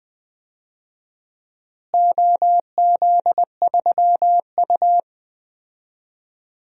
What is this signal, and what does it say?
Morse code, heard as a keyed tone, about 20 wpm, 700 Hz. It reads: OZ3U
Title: DXCC-100 Call Signs - 4 Characters - Repeated 3x - 20wpm